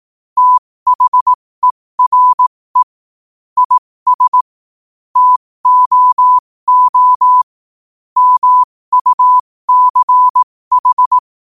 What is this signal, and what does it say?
Morse code, heard as a keyed tone, about 18 wpm, spaced as for 15 wpm, 1000 Hz. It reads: THERE IS TOO MUCH
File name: qotdfiles / 15wpm